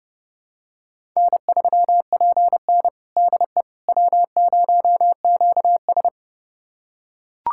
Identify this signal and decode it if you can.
Morse code, heard as a keyed tone, about 30 wpm, 700 Hz. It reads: N3PN DE W0QS
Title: USA Callsign DE Callsign - 4 Characters - 30wpm